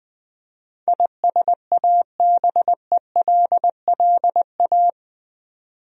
Morse code, heard as a keyed tone, about 21 wpm, 700 Hz. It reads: ISABELLA
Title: USA Common Names - 20wpm